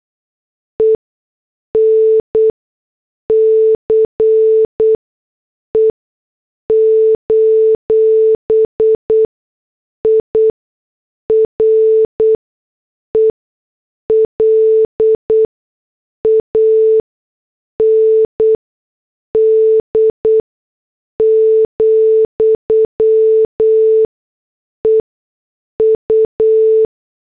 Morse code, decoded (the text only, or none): ENCE: IRELAND, EU